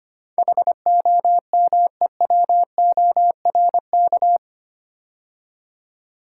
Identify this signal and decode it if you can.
Morse code, heard as a keyed tone, about 25 wpm, 700 Hz. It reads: HOMEWORK
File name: 8 Letter Words - First 3 Letters Given - 25wpm